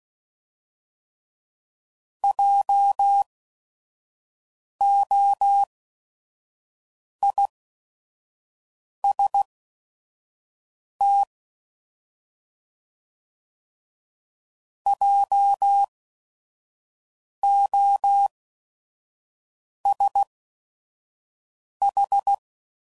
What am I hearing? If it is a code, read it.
Morse code, heard as a keyed tone, about 16 wpm, spaced as for 5 wpm, 800 Hz. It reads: JOIST JOSH